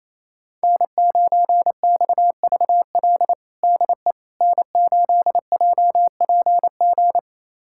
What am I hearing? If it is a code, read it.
Morse code, heard as a keyed tone, about 28 wpm, 700 Hz. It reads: N9XVL DE N8JPG
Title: USA Callsign DE Callsign - 5 Characters - 28wpm